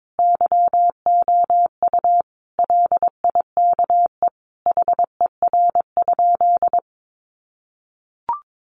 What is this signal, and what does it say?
Morse code, heard as a keyed tone, about 22 wpm, 700 Hz. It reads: YOU LIKE HER?
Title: Sentences from Top 100 Words 22wpm